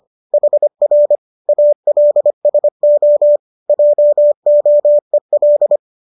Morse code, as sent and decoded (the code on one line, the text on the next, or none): .... .-. / .- .-.. ... --- / .--- --- . .-..
HR ALSO JOEL